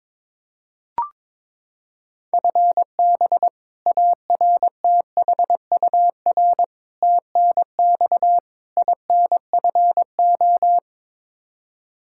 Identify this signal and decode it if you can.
Morse code, heard as a keyed tone, about 22 wpm, 700 Hz. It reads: FB ARTHUR TNX INFO
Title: Practice QSO - Exchange 3 - More Info - 22wpm